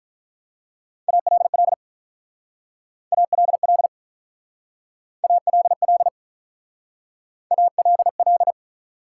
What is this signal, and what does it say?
Morse code, heard as a keyed tone, about 46 wpm, 700 Hz. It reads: ALL ALL ALL ALL